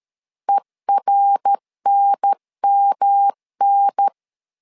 Morse code, heard as a keyed tone, about 13 wpm, 800 Hz. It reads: ERNMN
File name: LEZIONI / 3 MWRO / 3 MWR 12wpm 1711